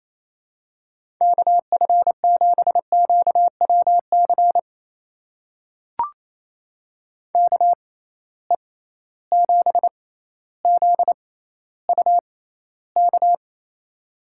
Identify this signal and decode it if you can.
Morse code, heard as a keyed tone, about 28 wpm, spaced as for 14 wpm, 700 Hz. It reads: KF7QWC KE7ZUK
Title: USA Call Signs - 6 Characters - 6x Character Spacing - 28wpm